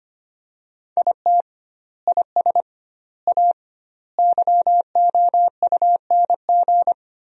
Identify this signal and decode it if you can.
Morse code, heard as a keyed tone, about 25 wpm, 700 Hz. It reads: IT IS A YOUNG
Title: 2x Word-Spacing - Sentences from Top 400 Words - 25wpm